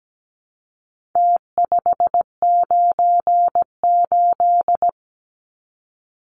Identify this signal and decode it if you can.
Morse code, heard as a keyed tone, about 17 wpm, 700 Hz. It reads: T598
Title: Numbers - Contest Style - 4-Digit 17wpm